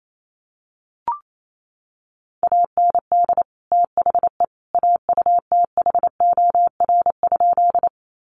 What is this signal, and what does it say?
Morse code, heard as a keyed tone, about 28 wpm, 700 Hz. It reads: AND THE AUTHOR?
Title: Sentences from Top 1200 Words 28wpm